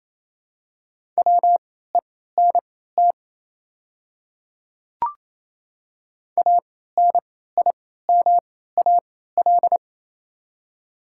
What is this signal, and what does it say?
Morse code, heard as a keyed tone, about 28 wpm, spaced as for 16 wpm, 700 Hz. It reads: WENT ANIMAL
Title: 3x Character Spacing - Top 200 Words - 28wpm